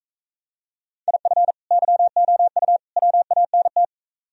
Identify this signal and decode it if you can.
Morse code, heard as a keyed tone, about 45 wpm, 700 Hz. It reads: IF YOU WANT